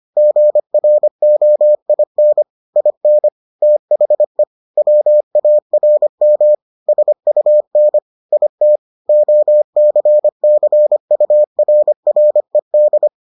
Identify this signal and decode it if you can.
Morse code, heard as a keyed tone, about 25 wpm, 600 Hz. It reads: GROIN IN THE WARM SUN IT OCCURRED